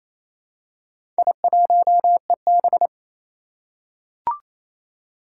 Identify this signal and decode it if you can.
Morse code, heard as a keyed tone, about 27 wpm, 700 Hz. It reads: I1EB
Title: Four Challenging Letter-Number Combinations 28wpm